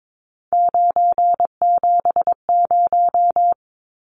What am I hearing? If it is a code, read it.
Morse code, heard as a keyed tone, about 22 wpm, 700 Hz. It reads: 970
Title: Numbers - 3-Digit 22wpm